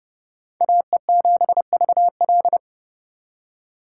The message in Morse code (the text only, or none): AE7VL